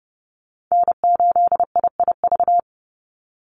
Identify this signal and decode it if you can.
Morse code, heard as a keyed tone, about 30 wpm, 700 Hz. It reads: N8IIV
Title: USA Call Signs 30wpm